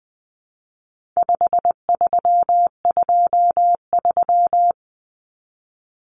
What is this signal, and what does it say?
Morse code, heard as a keyed tone, about 20 wpm, 700 Hz. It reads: 5323
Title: Numbers - 4-Digit 20wpm